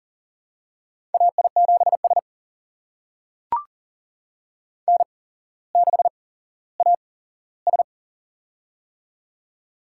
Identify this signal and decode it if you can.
Morse code, heard as a keyed tone, about 38 wpm, spaced as for 17 wpm, 700 Hz. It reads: AI7S N6AS
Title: USA Call Signs - 4 Characters - 8x Character Spacing - 40wpm